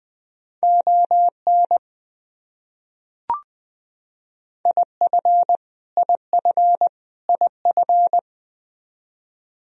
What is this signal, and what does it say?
Morse code, heard as a keyed tone, about 20 wpm, 700 Hz. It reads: ON IF IF IF
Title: Top 100 Words Repeated 20wpm